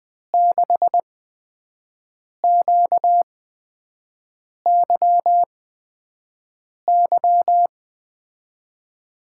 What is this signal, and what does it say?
Morse code, heard as a keyed tone, about 20 wpm, spaced as for 5 wpm, 700 Hz. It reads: 6QYY
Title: USA Call Signs - 5 Characters - 8x Character Spacing - 20wpm